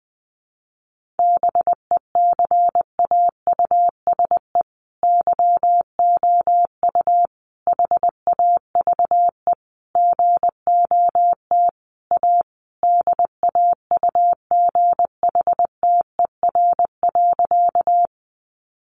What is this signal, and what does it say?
Morse code, heard as a keyed tone, about 20 wpm, 700 Hz. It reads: BECAUSE YOU HAVE GOT A DAUGHTER.
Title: Sentences from Top 1500 Words 20wpm